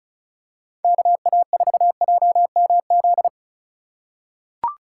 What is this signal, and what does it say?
Morse code, heard as a keyed tone, about 36 wpm, 700 Hz. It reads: KA4JMZ
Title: USA Call Signs - 6 Characters - 3x Character Spacing - 35wpm